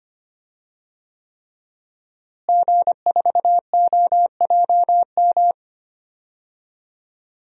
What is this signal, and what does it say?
Morse code, heard as a keyed tone, about 25 wpm, 700 Hz. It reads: G4OJM
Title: DXCC-100 Call Signs - 5 Characters - Repeated 3x - 25wpm